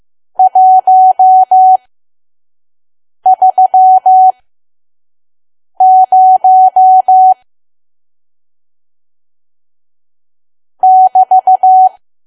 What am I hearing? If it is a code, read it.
Morse code, heard as a keyed tone, about 15 wpm, spaced as for 5 wpm, 750 Hz. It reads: 130 <BT>
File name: Archive / 5 WPM files / PM